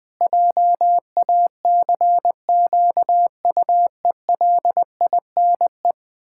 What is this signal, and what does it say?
Morse code, heard as a keyed tone, about 20 wpm, 700 Hz. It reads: JACQUELINE
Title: Practice QSO - Exchange 1 - Beginning - 20wpm